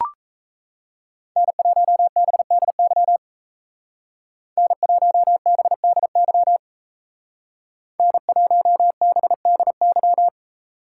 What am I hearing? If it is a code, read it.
Morse code, heard as a keyed tone, about 36 wpm, 700 Hz. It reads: N1BDY N1BDY N1BDY